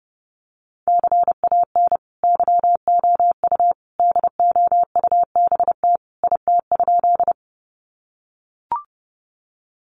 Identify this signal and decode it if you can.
Morse code, heard as a keyed tone, about 30 wpm, 700 Hz. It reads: CAN YOU DOUBT IT?
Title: Sentences from Top 1500 Words 30wpm